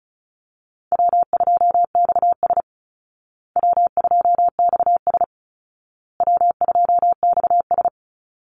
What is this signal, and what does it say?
Morse code, heard as a keyed tone, about 34 wpm, 700 Hz. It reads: W2XS W2XS W2XS
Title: Easy Call Signs Repeated 3x - 35wpm